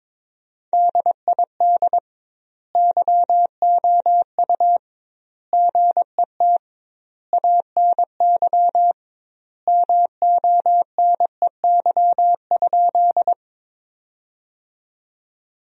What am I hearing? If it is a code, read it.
Morse code, heard as a keyed tone, about 22 wpm, 700 Hz. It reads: DID YOU GET ANY MONEY?
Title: 2x Word-Spacing - Sentences from Top 400 Words - 22wpm